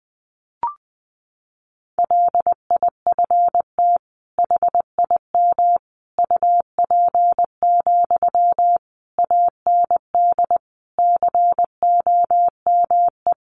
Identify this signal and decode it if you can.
Morse code, heard as a keyed tone, about 20 wpm, 700 Hz. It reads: LIFT HIM UP, AND COME
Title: Sentences from Top 1000 Words 20wpm